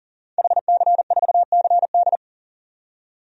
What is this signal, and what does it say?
Morse code, heard as a keyed tone, about 40 wpm, 700 Hz. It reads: HC4CD
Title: DXCC-100 Call Signs - 5 Characters - 40wpm